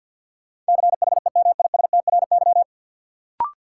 Morse code, heard as a keyed tone, about 52 wpm, 700 Hz. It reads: CHEMISTRY
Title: Sentences from Top 1000 Words - Review Part 6 - New Word Drill - 50wpm